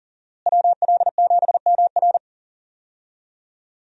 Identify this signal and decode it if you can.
Morse code, heard as a keyed tone, about 40 wpm, 700 Hz. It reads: WL7MR
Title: USA Call Signs - 5 Characters - Repeated 3x - 40wpm